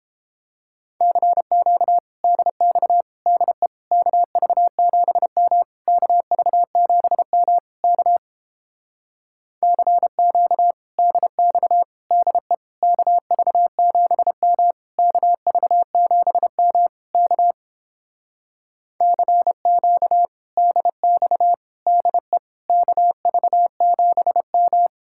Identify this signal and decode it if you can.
Morse code, heard as a keyed tone, about 29 wpm, 700 Hz. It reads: CQ DX DE KV7M KV7M K CQ DX DE KV7M KV7M K CQ DX DE KV7M